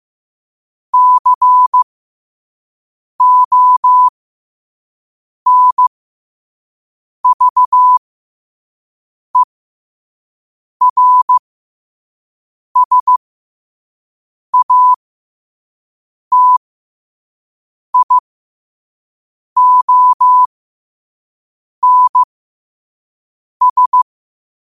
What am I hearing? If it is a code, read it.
Morse code, heard as a keyed tone, about 15 wpm, spaced as for 5 wpm, 1000 Hz. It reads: CONVERSATIONS
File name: qotdfiles / 5wpm